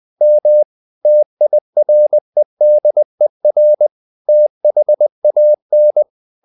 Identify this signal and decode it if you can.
Morse code, heard as a keyed tone, about 20 wpm, 600 Hz. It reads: M TIREDER THAN
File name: the old man and the sea part 2 20wpm